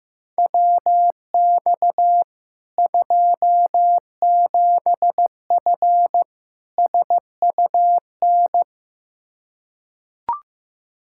Morse code, heard as a keyed tone, about 15 wpm, 700 Hz. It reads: WX 27F SUN
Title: Practice QSO - Exchange 2 - Info - 15wpm